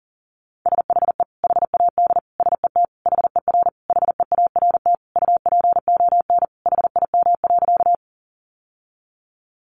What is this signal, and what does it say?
Morse code, heard as a keyed tone, about 40 wpm, 700 Hz. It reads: SHE HAD SET HER HEART UPON HIM.
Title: Sentences from Top 1200 Words 40wpm